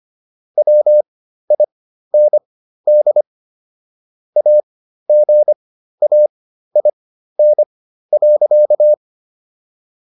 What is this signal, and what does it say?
Morse code, heard as a keyed tone, about 25 wpm, spaced as for 13 wpm, 600 Hz. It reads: WIND AGAIN.